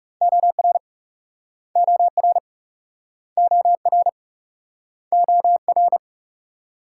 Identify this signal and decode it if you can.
Morse code, heard as a keyed tone, about 40 wpm, 700 Hz. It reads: OR OR OR OR